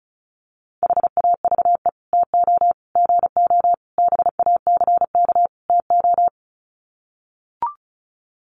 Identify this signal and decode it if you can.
Morse code, heard as a keyed tone, about 34 wpm, 700 Hz. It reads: HAVE TO GO BACK TO
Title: Sets of 5 Words Encore2 35wpm